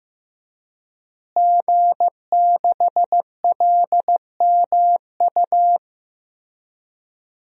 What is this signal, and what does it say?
Morse code, heard as a keyed tone, about 15 wpm, 700 Hz. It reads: G6LMU
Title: DXCC-100 Call Signs - 5 Characters - Repeated 3x - 15wpm